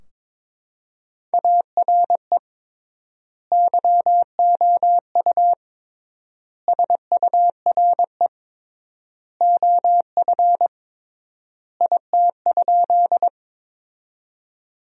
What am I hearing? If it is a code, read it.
Morse code, heard as a keyed tone, about 22 wpm, 700 Hz. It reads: ARE YOU SURE OF IT?
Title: 3x Word-Spacing - Sentences from Top 300 Words - 22wpm